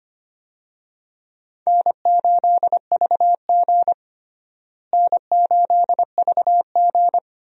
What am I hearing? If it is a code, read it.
Morse code, heard as a keyed tone, about 25 wpm, 700 Hz. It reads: N8VG N8VG
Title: USA Call Signs - 4 Characters - Repeated 2x - 25wpm